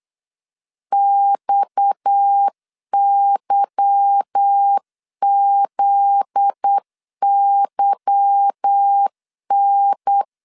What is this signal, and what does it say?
Morse code, heard as a keyed tone, about 8 wpm, 800 Hz. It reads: XYZYN
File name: LEZIONI / 8 YJ45 / 8 Y 8wpm 3245